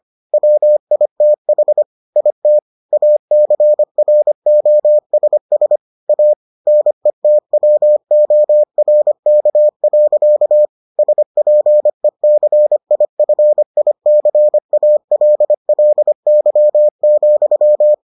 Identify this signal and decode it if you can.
Morse code, heard as a keyed tone, about 25 wpm, 600 Hz. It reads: WITH IT ACROSS A NETWORK. SPECIFICALLY,